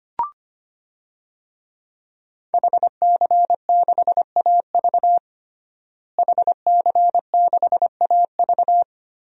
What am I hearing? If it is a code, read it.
Morse code, heard as a keyed tone, about 25 wpm, 700 Hz. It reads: HC6AV HC6AV
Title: DXCC-100 Call Signs - 5 Characters - Repeated 2x - 25wpm